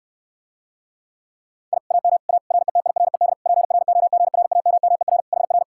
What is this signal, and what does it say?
Morse code, heard as a keyed tone, about 69 wpm, 700 Hz. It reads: I AM A RETIRED PROGRAMMER HR